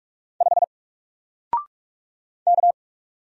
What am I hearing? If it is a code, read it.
Morse code, heard as a keyed tone, about 46 wpm, 700 Hz. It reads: 5 K
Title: Single Letter-Number Alt 45wpm